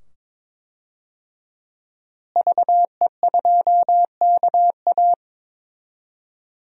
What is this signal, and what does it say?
Morse code, heard as a keyed tone, about 22 wpm, 700 Hz. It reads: VE2KA
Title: DXCC-100 Call Signs - 5 Characters - Repeated 2x - 22wpm